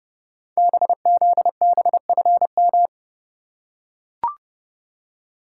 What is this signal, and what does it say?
Morse code, heard as a keyed tone, about 30 wpm, 700 Hz. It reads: BZBFM
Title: Code Groups - 5 Letters - 30wpm